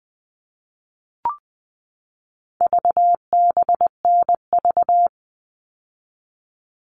Morse code, heard as a keyed tone, about 20 wpm, 700 Hz. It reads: VBNV
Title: All Cut Numbers - 4-Digit 20wpm